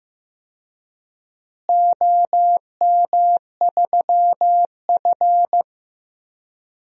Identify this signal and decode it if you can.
Morse code, heard as a keyed tone, about 15 wpm, 700 Hz. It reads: OM3F
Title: DXCC-100 Call Signs - 4 Characters - Repeated 2x - 15wpm